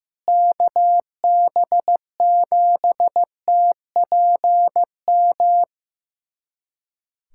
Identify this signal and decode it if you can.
Morse code, heard as a keyed tone, about 15 wpm, 700 Hz. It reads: KB7TPM
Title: USA Callsign DE Callsign - 6 Characters - 15wpm